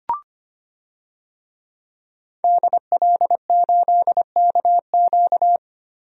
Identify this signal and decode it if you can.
Morse code, heard as a keyed tone, about 25 wpm, 700 Hz. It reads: DL8KQ
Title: DXCC-100 Call Signs - 5 Characters - Repeated 3x - 25wpm